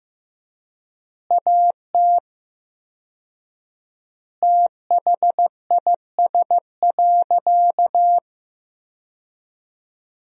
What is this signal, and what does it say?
Morse code, heard as a keyed tone, about 16 wpm, 700 Hz. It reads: AT THIS.
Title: 4x Word-Spacing - Sentences from Top 300 Words - 15wpm